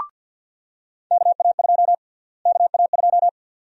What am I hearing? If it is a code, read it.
Morse code, heard as a keyed tone, about 52 wpm, 700 Hz. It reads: KA2 KA2
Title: USA Call Sign Prefixes - Repeated 2x - 50wpm